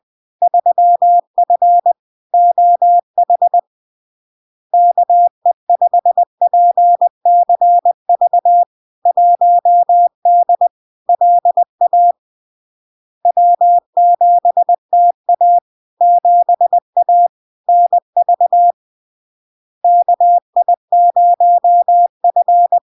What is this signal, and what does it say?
Morse code, heard as a keyed tone, about 20 wpm, 700 Hz. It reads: 3F OH KE5PCV 1D LA W7TA 7A NV KI0F
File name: LARK-FD-2022-Log-in-CW-20wpm-700-Hz